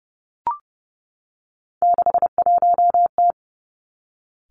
Morse code, heard as a keyed tone, about 30 wpm, 700 Hz. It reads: 61T
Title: Three Challenging Letter-Number Combinations 30wpm